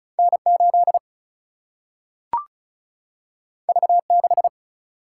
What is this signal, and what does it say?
Morse code, heard as a keyed tone, about 36 wpm, 700 Hz. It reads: N 8 V 6
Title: Single-Letter Single-Number 35wpm